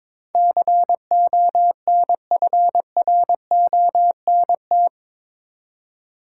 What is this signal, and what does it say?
Morse code, heard as a keyed tone, about 22 wpm, 700 Hz. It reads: CONFRONT
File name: practice / 22wpm / Eight Letter Words - 22wpm